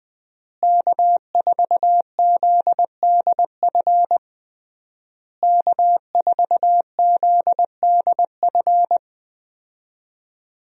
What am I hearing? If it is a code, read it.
Morse code, heard as a keyed tone, about 21 wpm, 700 Hz. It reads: K4ZDF K4ZDF
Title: USA Call Signs - 5 Characters - Repeated 3x - 20wpm